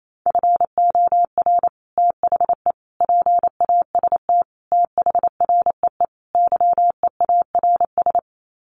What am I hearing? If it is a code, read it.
Morse code, heard as a keyed tone, about 28 wpm, 700 Hz. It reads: FOR THE PAST THREE YEARS
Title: Sets of 5 Words Encore2 28wpm